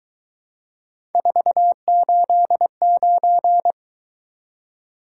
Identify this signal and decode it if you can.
Morse code, heard as a keyed tone, about 22 wpm, 700 Hz. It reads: 489